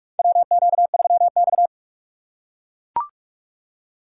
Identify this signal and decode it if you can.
Morse code, heard as a keyed tone, about 41 wpm, 700 Hz. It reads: WQ3X
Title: USA Call Signs - 4 Characters - 2x Character Spacing - 45wpm